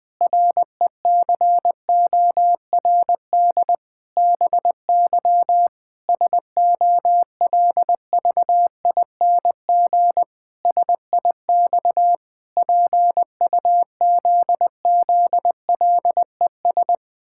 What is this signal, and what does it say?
Morse code, heard as a keyed tone, about 21 wpm, 700 Hz. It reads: RECORD BY SOLVING SIX PUZZLES
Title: Pangram Challenge v2 - 20wpm